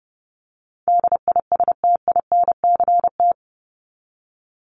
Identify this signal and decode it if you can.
Morse code, heard as a keyed tone, about 30 wpm, 700 Hz. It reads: DISTINCT
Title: Top 1501-1750 Words 30wpm